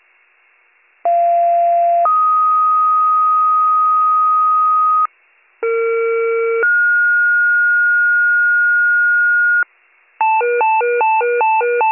Two-tone page, heard as a one-second tone, then a three-second tone, about 690 then 1250 Hz, about 460 then 1430 Hz, with noise. It is heard from 1.0 to 9.6 s.